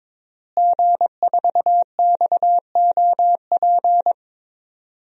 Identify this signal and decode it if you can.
Morse code, heard as a keyed tone, about 22 wpm, 700 Hz. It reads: G4XOP
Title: DXCC-100 Call Signs - 5 Characters - Repeated 2x - 22wpm